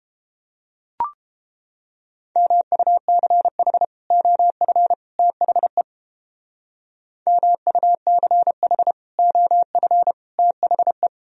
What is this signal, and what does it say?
Morse code, heard as a keyed tone, about 32 wpm, 700 Hz. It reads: MUCH OF THE MUCH OF THE